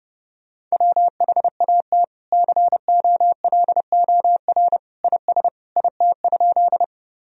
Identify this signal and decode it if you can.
Morse code, heard as a keyed tone, about 32 wpm, 700 Hz. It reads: WHAT COLOR IS IT?
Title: Sentences from Top 1000 Words - Review Part 2 - 30wpm